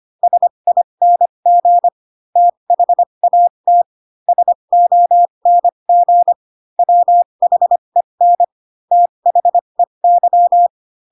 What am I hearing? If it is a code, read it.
Morse code, heard as a keyed tone, about 25 wpm, spaced as for 22 wpm, 700 Hz. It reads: SING THAT SONG WHEN THEY